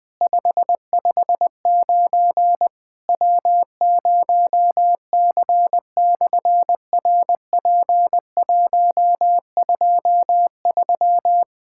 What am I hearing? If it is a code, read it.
Morse code, heard as a keyed tone, about 20 wpm, 700 Hz. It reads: 559 W0C/RP123
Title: SOTA QSO Practice 20wpm